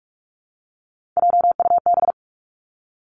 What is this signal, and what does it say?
Morse code, heard as a keyed tone, about 46 wpm, 700 Hz. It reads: JUB
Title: Three Challenging Letter-Number Combinations 45wpm